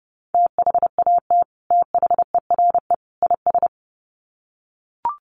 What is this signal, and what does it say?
Morse code, heard as a keyed tone, about 30 wpm, 700 Hz. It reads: THAT THERE IS
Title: Sets of 3 Words 30wpm